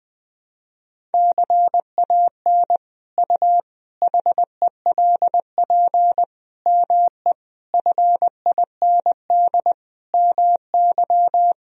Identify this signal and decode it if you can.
Morse code, heard as a keyed tone, about 20 wpm, 700 Hz. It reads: CAN U HELP ME FIND MY
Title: Sentences from Top 200 Words and Abbreviations 20wpm